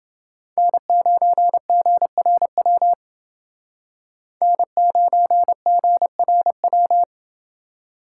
Morse code, heard as a keyed tone, about 28 wpm, 700 Hz. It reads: N9GRW N9GRW